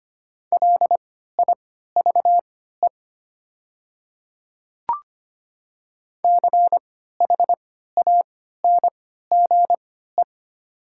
Morse code, heard as a keyed tone, about 25 wpm, spaced as for 14 wpm, 700 Hz. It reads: LIVE CHANGE
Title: 3x Character Spacing - Top 200 Words - 25wpm